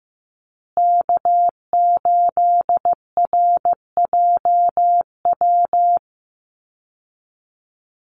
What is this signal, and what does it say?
Morse code, heard as a keyed tone, about 15 wpm, 700 Hz. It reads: K8RJW
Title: Call Signs Repeated 3x - 15wpm